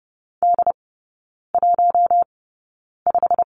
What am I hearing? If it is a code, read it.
Morse code, heard as a keyed tone, about 30 wpm, spaced as for 9 wpm, 700 Hz. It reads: D15
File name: course / 30:10wpm / 087 Lesson 15e - Number 5 - Three Characters 30wpm